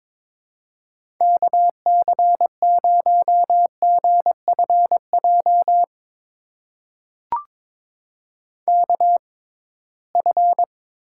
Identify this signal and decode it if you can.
Morse code, heard as a keyed tone, about 22 wpm, spaced as for 17 wpm, 700 Hz. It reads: KC0GFJ KF
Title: USA Call Signs - 6 Characters - 6x Character Spacing - 22wpm